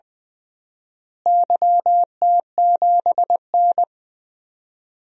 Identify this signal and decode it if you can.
Morse code, heard as a keyed tone, about 20 wpm, 700 Hz. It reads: YT7N
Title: DXCC-100 Call Signs - 4 Characters - Repeated 3x - 20wpm